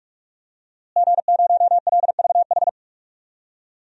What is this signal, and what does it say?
Morse code, heard as a keyed tone, about 42 wpm, 700 Hz. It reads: G0LVH
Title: DXCC-100 Call Signs - 5 Characters - Repeated 2x - 45wpm